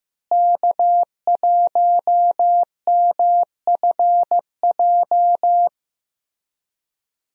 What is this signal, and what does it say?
Morse code, heard as a keyed tone, about 15 wpm, 700 Hz. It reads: K1MFJ
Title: USA Call Signs - 5 Characters - Repeated 3x - 15wpm